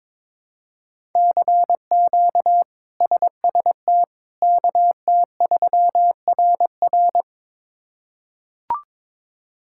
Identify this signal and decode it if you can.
Morse code, heard as a keyed tone, about 21 wpm, 700 Hz. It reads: CQ SST KT3RR
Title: K1USN SST QSO Practice - 22wpm